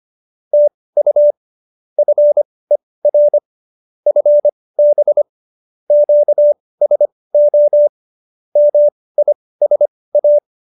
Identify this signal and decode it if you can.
Morse code, heard as a keyed tone, about 25 wpm, spaced as for 18 wpm, 590 Hz. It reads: TU FER FB QSO MISA